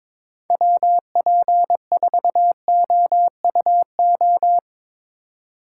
Morse code, heard as a keyed tone, about 24 wpm, 700 Hz. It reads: WP4OUO
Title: Speed Racing - USA 2x3 Call Signs 22wpm